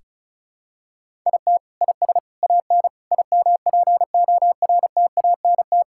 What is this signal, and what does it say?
Morse code, heard as a keyed tone, about 36 wpm, 700 Hz. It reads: IT IS AN IMPORTANT
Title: Sentences from Top 1000 Words - Review Part 5 - 35wpm